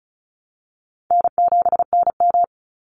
Morse code, heard as a keyed tone, about 34 wpm, 700 Hz. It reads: N7NM
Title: Call Signs - 35wpm